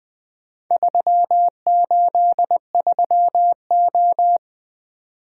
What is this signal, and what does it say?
Morse code, heard as a keyed tone, about 20 wpm, 700 Hz. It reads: 383O
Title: Four Letter-Number Combinations - 20wpm